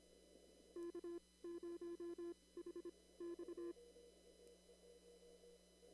Morse code, heard as a keyed tone, about 25 wpm, spaced as for 18 wpm, 1700 Hz, with noise. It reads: K0HX
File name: CW / kt5j wpx cw-op-k5pi